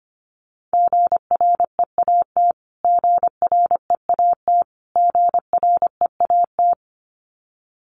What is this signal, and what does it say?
Morse code, heard as a keyed tone, about 25 wpm, 700 Hz. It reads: GREAT GREAT GREAT
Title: Top 200 Words Repeated 25wpm